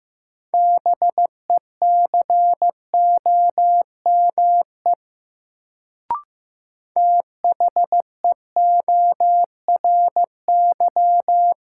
Morse code, heard as a keyed tone, about 15 wpm, 700 Hz. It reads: BECOME THEORY
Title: Sentences from Top 600 Words - New Word Drill - 15wpm